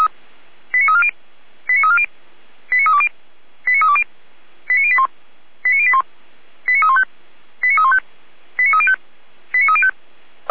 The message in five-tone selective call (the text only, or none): 89340 89340 89320 89320 89081 89081 89315 89315 89305 89305